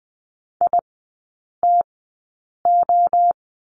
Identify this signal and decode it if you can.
Morse code, heard as a keyed tone, about 20 wpm, spaced as for 8 wpm, 700 Hz. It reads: ITO
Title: 067 Lesson 12e - Letter d - Three Characters 20wpm